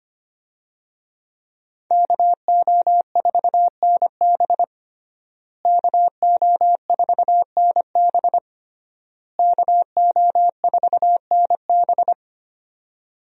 Morse code, heard as a keyed tone, about 25 wpm, 700 Hz. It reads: KO4NB KO4NB KO4NB
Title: USA Call Signs - 5 Characters - Repeated 3x - 25wpm